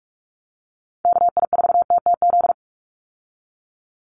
Morse code, heard as a keyed tone, about 44 wpm, 700 Hz. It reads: KI4TTZ
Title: USA Call Signs 45wpm